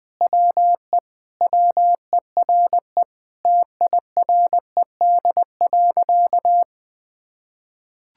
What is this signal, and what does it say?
Morse code, heard as a keyed tone, about 21 wpm, 700 Hz. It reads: WE WERE TIRED.
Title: Sentences with 5-Word Phrases - 20wpm